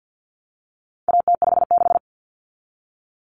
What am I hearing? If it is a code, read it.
Morse code, heard as a keyed tone, about 50 wpm, 700 Hz. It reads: AT56
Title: Common Cut Numbers - 4-Digit 50wpm